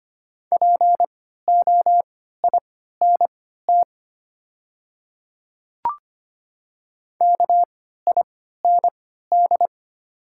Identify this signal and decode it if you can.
Morse code, heard as a keyed tone, about 25 wpm, spaced as for 14 wpm, 700 Hz. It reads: POINT KIND
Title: 3x Character Spacing - Top 300 Words - 25wpm